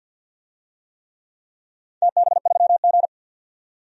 Morse code, heard as a keyed tone, about 47 wpm, 700 Hz. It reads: TB3G
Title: DXCC-100 Call Signs - 4 Characters - 50wpm